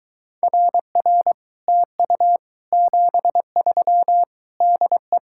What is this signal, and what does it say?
Morse code, heard as a keyed tone, about 23 wpm, 700 Hz. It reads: RR TU 73 DE